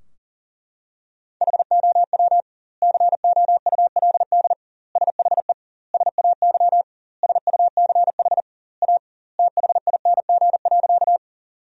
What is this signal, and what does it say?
Morse code, heard as a keyed tone, about 40 wpm, 700 Hz. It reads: HOW COULD SHE SAY SUCH A THING.
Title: 2x Word-Spacing - Sentences from Top 200 Words - 40wpm